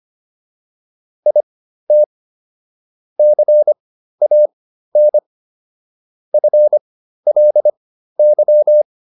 Morse code, heard as a keyed tone, about 25 wpm, spaced as for 13 wpm, 600 Hz. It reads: IT CAN FLY